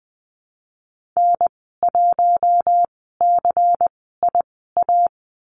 Morse code, heard as a keyed tone, about 19 wpm, spaced as for 14 wpm, 700 Hz. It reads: N1CIA
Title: 173 Lesson 28g - Number 9 - Call Signs 20wpm